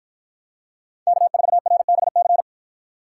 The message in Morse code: K4RBC

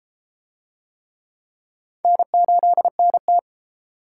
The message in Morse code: N8NT

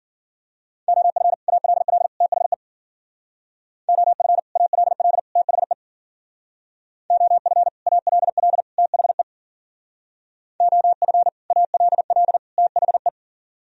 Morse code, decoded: OF ALL THE OF ALL THE OF ALL THE OF ALL THE

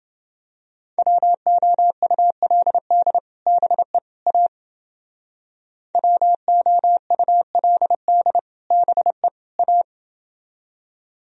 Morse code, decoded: WOULD BE A WOULD BE A

